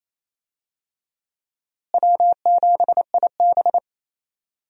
Morse code, heard as a keyed tone, about 28 wpm, 700 Hz. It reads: W7IB
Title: USA Call Signs - 4 Characters - Repeated 2x - 28wpm